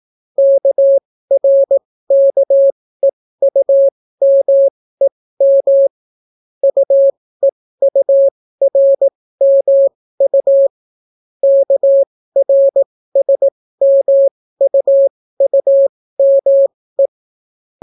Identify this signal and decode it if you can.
Morse code, heard as a keyed tone, about 18 wpm, spaced as for 15 wpm, 550 Hz. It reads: KRKEUMEM UEURMU KRSMUUME